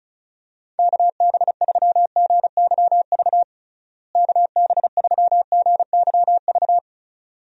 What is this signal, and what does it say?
Morse code, heard as a keyed tone, about 36 wpm, 700 Hz. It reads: KB3GYV KB3GYV
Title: USA Call Signs - 6 Characters - Repeated 2x - 35wpm